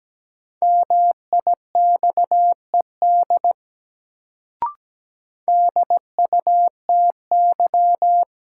Morse code, heard as a keyed tone, about 17 wpm, 700 Hz. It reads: MIXED DUTY
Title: Sentences from Top 1500 Words - New Word Drill Alt - 17wpm